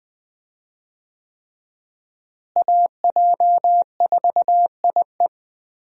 Morse code, heard as a keyed tone, about 20 wpm, 700 Hz. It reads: AJ4IE
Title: USA Call Signs - 5 Characters - Repeated 3x - 20wpm